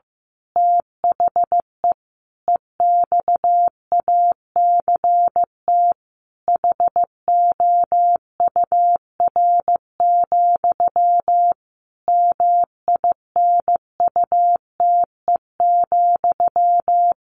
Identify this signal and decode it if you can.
Morse code, heard as a keyed tone, about 15 wpm, 700 Hz. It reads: THE EXACT HOUR, MINUTE,
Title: A Sound Of Thunder 15wpm